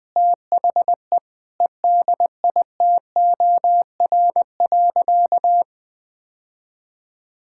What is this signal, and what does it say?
Morse code, heard as a keyed tone, about 21 wpm, 700 Hz. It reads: THE EDITOR.
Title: Sentences with 6-Word Phrases - 20wpm